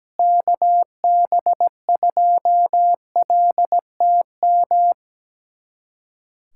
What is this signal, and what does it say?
Morse code, heard as a keyed tone, about 17 wpm, 700 Hz. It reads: KB2LTM
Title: USA Callsign DE Callsign 17wpm